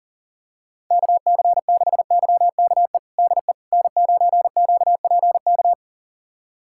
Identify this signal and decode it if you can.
Morse code, heard as a keyed tone, about 40 wpm, 700 Hz. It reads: KC6YKE DE N9QPK
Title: USA Callsign DE Callsign 40wpm